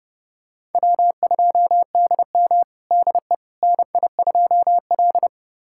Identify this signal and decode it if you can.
Morse code, heard as a keyed tone, about 31 wpm, 700 Hz. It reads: W2DM DE NI2L
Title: USA Callsign DE Callsign - 4 Characters - 30wpm